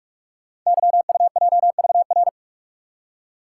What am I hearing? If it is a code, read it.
Morse code, heard as a keyed tone, about 44 wpm, 700 Hz. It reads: YUJVR
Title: Code Groups - 5 Letters - 45wpm